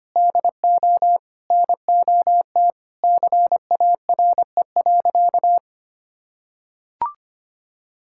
Morse code, heard as a keyed tone, about 25 wpm, 700 Hz. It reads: DO NOT CARE.
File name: practice / 25wpm / Sentences with 4-Word Phrases - 25wpm